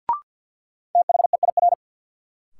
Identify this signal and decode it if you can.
Morse code, heard as a keyed tone, about 47 wpm, 700 Hz. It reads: THEIR
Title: Sentences from Top 300 Words - New Word Drill - 50wpm